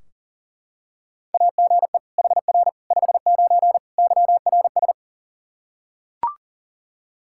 Morse code, heard as a keyed tone, about 40 wpm, 700 Hz. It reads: AGE HR 59 YRS
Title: Practice QSO - Exchange 3 - More Info - 40wpm